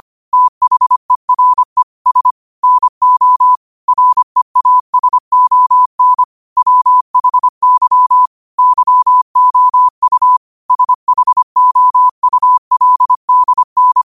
Morse code, heard as a keyed tone, about 25 wpm, 1000 Hz. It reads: THERE S NO REASON WHY YOU SHOULDN
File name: qotdfiles / 25wpm